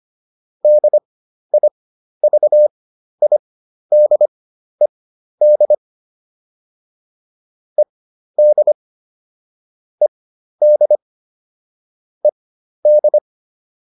Morse code, heard as a keyed tone, about 25 wpm, spaced as for 12 wpm, 600 Hz. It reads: DIVIDED ED ED ED